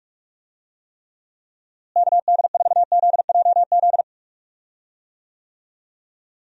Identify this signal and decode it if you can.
Morse code, heard as a keyed tone, about 46 wpm, 700 Hz. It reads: KD4ZJZ
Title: USA Call Signs - 6 Characters - Repeated 2x - 45wpm